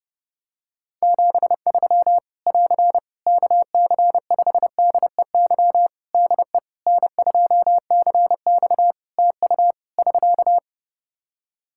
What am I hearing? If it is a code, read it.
Morse code, heard as a keyed tone, about 30 wpm, 700 Hz. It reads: 73 <AR> KC5DEY DE N2CX TU <SK>
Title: Practice QSO - Exchange 4 - Ending - 30wpm